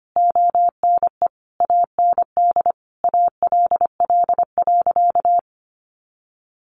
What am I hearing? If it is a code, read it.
Morse code, heard as a keyed tone, about 25 wpm, 700 Hz. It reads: ONE AND ALL.
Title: Sentences from Top 200 Words 25wpm